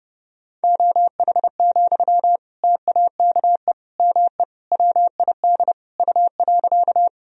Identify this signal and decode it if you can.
Morse code, heard as a keyed tone, about 31 wpm, 700 Hz. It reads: OH, TAKE ME WID U.
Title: Sentences from Top 500 Words and Abbreviations 30wpm